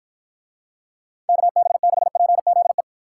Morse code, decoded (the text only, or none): KB6PZE